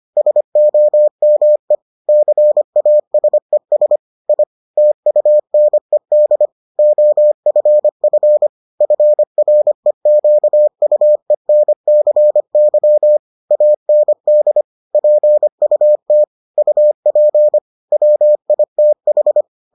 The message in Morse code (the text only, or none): SOME CASES I TUNED OFF FREQUENCY AND PUT UP WITH